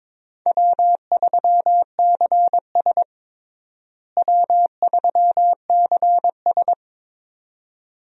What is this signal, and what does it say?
Morse code, heard as a keyed tone, about 22 wpm, 700 Hz. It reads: W3CS W3CS
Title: USA Call Signs - 4 Characters - Repeated 2x - 22wpm